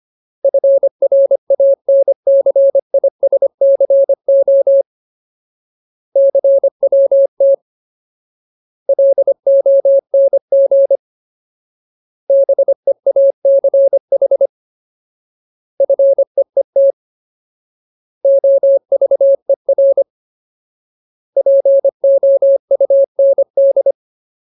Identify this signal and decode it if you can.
Morse code, heard as a keyed tone, about 25 wpm, 550 Hz. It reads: FRANCISCO CWT LONG BEACH FEET OVER POUND
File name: Ultimate Word List / 25WPM Ultimate